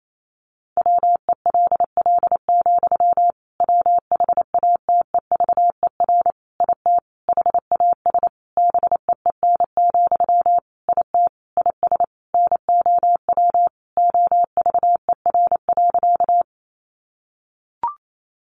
Morse code, heard as a keyed tone, about 28 wpm, 700 Hz. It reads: WELL, WHATEVER IT HAS BEEN, IT IS NOW OVER.
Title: Sentences from Top 1100 Words 28wpm